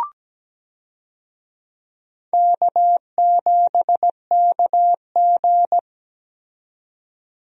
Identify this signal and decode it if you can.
Morse code, heard as a keyed tone, about 17 wpm, 700 Hz. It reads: K7KG
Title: USA Call Signs - 4 Characters - 17wpm